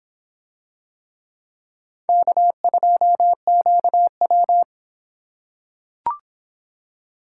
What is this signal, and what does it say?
Morse code, heard as a keyed tone, about 24 wpm, 700 Hz. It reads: K2QW